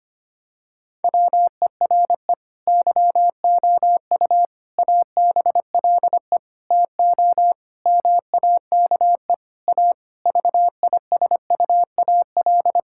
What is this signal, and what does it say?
Morse code, heard as a keyed tone, about 25 wpm, 700 Hz. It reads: WERE YOU ABLE TO MAKE A VISUAL
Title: Sentences from Top 1000 Words - Review Part 6 - 25wpm